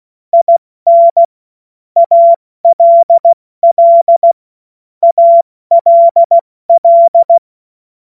Morse code, heard as a keyed tone, about 16 wpm, 680 Hz. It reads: IN ALL ALL
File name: Morsekurs / Witze / Witz 23 Januar 2024 70 F80